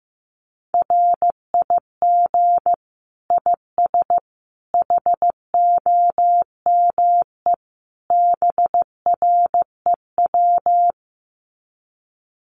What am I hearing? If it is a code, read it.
Morse code, heard as a keyed tone, about 15 wpm, 700 Hz. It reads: RIG IS HOME BREW
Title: Common QSO Phrases 15wpm